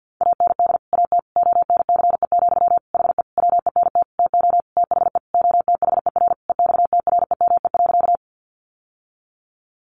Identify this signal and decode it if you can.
Morse code, heard as a keyed tone, about 50 wpm, 700 Hz. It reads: AND AT ONCE, HE WENT TO THE OTHER EXTREME.
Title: Sentences from Top 1300 Words 50wpm